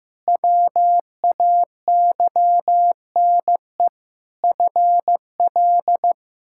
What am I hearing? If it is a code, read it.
Morse code, heard as a keyed tone, about 15 wpm, 700 Hz. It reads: WAYNE FL
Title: K1USN SST QSO Practice - 15wpm